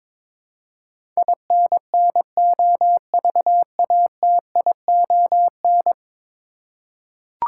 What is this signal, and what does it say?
Morse code, heard as a keyed tone, about 22 wpm, 700 Hz. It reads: INNOVATION
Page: morse-code-ninja-practice-22wpm